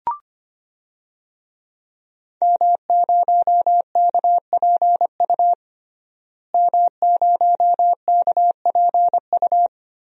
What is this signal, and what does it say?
Morse code, heard as a keyed tone, about 25 wpm, 700 Hz. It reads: M0KPU M0KPU
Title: DXCC-100 Call Signs - 5 Characters - Repeated 3x - 25wpm